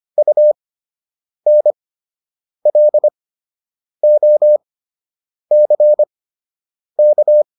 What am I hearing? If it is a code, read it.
Morse code, heard as a keyed tone, about 25 wpm, spaced as for 8 wpm, 600 Hz. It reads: UNLOCK